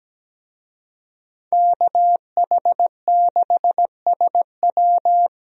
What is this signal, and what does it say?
Morse code, heard as a keyed tone, about 17 wpm, 700 Hz. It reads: KH6SW
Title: DXCC-100 Call Signs - 5 Characters - Repeated 2x - 17wpm